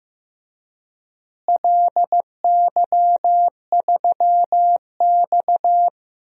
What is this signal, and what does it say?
Morse code, heard as a keyed tone, about 15 wpm, 700 Hz. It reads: LY3X
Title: DXCC-100 Call Signs - 4 Characters - Repeated 3x - 15wpm